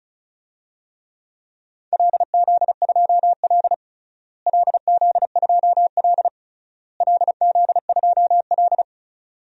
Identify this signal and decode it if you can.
Morse code, heard as a keyed tone, about 36 wpm, 700 Hz. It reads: LZ2L LZ2L LZ2L
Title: DXCC-100 Call Signs - 4 Characters - Repeated 3x - 35wpm